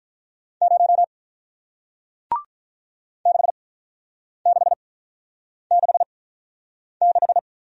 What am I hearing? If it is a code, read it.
Morse code, heard as a keyed tone, about 48 wpm, spaced as for 9 wpm, 700 Hz. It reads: 0 6666